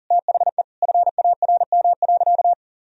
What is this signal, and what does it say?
Morse code, heard as a keyed tone, about 37 wpm, 700 Hz. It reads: THE FARM.
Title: Sentences from Top 1000 Words - Review Part 2 - 40wpm